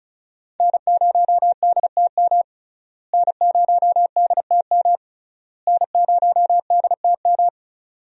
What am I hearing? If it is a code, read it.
Morse code, heard as a keyed tone, about 36 wpm, 700 Hz. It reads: N0DTM N0DTM N0DTM
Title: USA Call Signs - 5 Characters - Repeated 3x - 35wpm